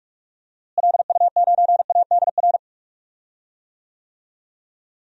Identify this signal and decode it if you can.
Morse code, heard as a keyed tone, about 42 wpm, 700 Hz. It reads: LU9ADR
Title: DXCC-100 Call Signs - 6 Characters - Repeated 2x - 45wpm